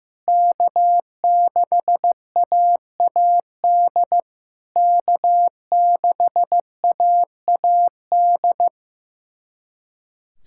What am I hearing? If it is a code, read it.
Morse code, heard as a keyed tone, about 15 wpm, 700 Hz. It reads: K6AAD K6AAD
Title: SOTA QSO Practice 15wpm